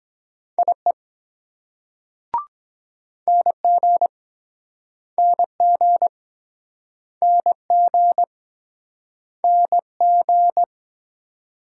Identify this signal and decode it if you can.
Morse code, heard as a keyed tone, about 23 wpm, 700 Hz. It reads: IE NG NG NG NG